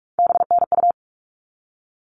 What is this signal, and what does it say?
Morse code, heard as a keyed tone, about 44 wpm, 700 Hz. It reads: BNU
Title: All Cut Numbers - 3-Digit 45wpm